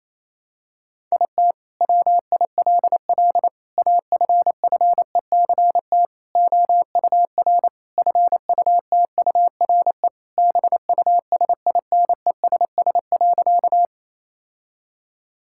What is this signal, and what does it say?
Morse code, heard as a keyed tone, about 28 wpm, 700 Hz. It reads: IT WILL AFFECT OUR FUTURE BUSINESS.
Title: Sentences from Top 1000 Words - Review Part 6 - 28wpm